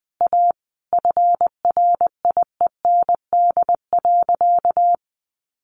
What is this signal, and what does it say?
Morse code, heard as a keyed tone, about 20 wpm, 700 Hz. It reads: A FRIEND.
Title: Sentences from Top 1000 Words 20wpm